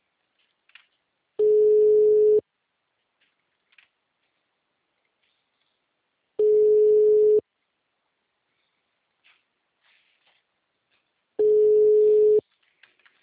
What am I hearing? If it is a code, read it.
Ringback tone (Europe, 425 Hz).
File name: out- -2013- - - .35873